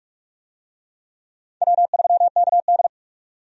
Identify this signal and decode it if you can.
Morse code, heard as a keyed tone, about 43 wpm, 700 Hz. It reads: W3KD
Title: USA Call Signs - 4 Characters - 45wpm